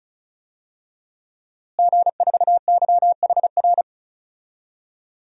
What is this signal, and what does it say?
Morse code, heard as a keyed tone, about 34 wpm, 700 Hz. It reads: G4YHR
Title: DXCC-100 Call Signs - 5 Characters - Repeated 3x - 35wpm